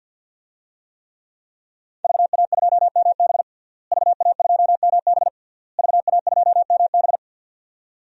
Morse code, heard as a keyed tone, about 52 wpm, 700 Hz. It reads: VA2MB VA2MB VA2MB
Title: DXCC-100 Call Signs - 5 Characters - Repeated 3x - 50wpm